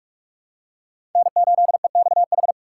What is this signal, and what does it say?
Morse code, heard as a keyed tone, about 44 wpm, 700 Hz. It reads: N8EXH
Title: USA Call Signs - 3x Character Spacing - 45wpm